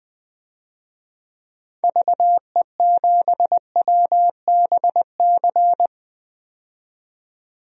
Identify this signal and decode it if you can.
Morse code, heard as a keyed tone, about 20 wpm, 700 Hz. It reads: VE7WBC
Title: DXCC-100 Call Signs - 6 Characters - 20wpm